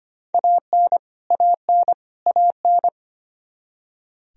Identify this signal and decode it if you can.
Morse code, heard as a keyed tone, about 25 wpm, 700 Hz. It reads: AN AN AN
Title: Top 100 Words Repeated 25wpm